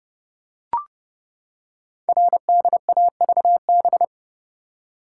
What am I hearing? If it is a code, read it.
Morse code, heard as a keyed tone, about 30 wpm, 700 Hz. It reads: RDAVB
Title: Code Groups - 5 Letters - 30wpm